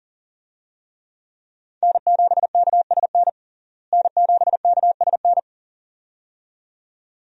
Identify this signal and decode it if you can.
Morse code, heard as a keyed tone, about 39 wpm, 700 Hz. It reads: N7KSN N7KSN
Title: USA Call Signs - 5 Characters - Repeated 2x - 40wpm